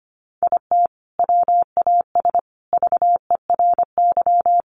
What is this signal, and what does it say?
Morse code, heard as a keyed tone, about 25 wpm, 700 Hz. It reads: IT WAS VERY
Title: Sentences from Top 1100 Words 25wpm